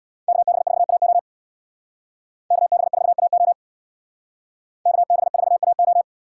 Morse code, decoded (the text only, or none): KB4AO KB4AO KB4AO